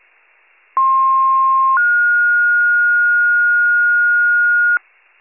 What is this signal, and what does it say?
Two-tone page, heard as a one-second tone, then a three-second tone, about 1060 then 1430 Hz, with noise. It is heard from 0.8 to 4.8 s.